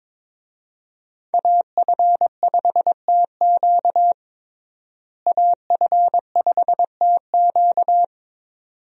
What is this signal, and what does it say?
Morse code, heard as a keyed tone, about 21 wpm, 700 Hz. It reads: AF5TQ AF5TQ
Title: USA Call Signs - 5 Characters - Repeated 2x - 22wpm